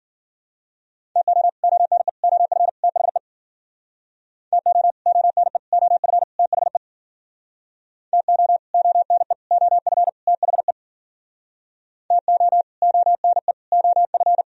Morse code, heard as a keyed tone, about 52 wpm, 700 Hz. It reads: TO ONE OF THE TO ONE OF THE TO ONE OF THE TO ONE OF